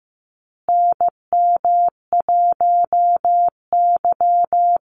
Morse code, heard as a keyed tone, about 15 wpm, 700 Hz. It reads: NM1Y
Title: Call Signs Repeated 2x - 15wpm